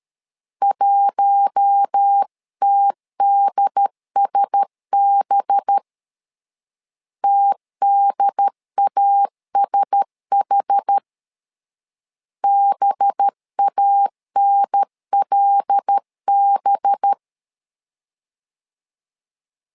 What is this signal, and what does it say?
Morse code, heard as a keyed tone, about 13 wpm, 800 Hz. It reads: 1TDSB TDASH BANLB
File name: LEZIONI / 5 LBVC / 5 LB 12wpm 1906